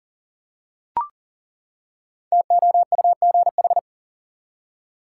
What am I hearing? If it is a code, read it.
Morse code, heard as a keyed tone, about 40 wpm, 700 Hz. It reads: TOUGH
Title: Five Letter Words - 40wpm